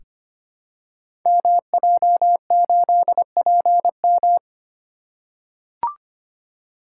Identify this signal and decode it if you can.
Morse code, heard as a keyed tone, about 25 wpm, 700 Hz. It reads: MJ8PM
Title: Code Groups - 5 Letter:Numbers - 25wpm